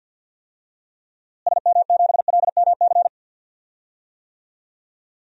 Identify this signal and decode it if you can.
Morse code, heard as a keyed tone, about 49 wpm, 700 Hz. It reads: SM7LMC
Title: DXCC-100 Call Signs - 6 Characters - Repeated 2x - 50wpm